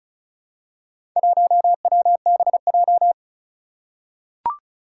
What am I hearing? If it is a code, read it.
Morse code, heard as a keyed tone, about 36 wpm, 700 Hz. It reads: 1WBJ
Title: Four Challenging Letter-Number Combinations 35wpm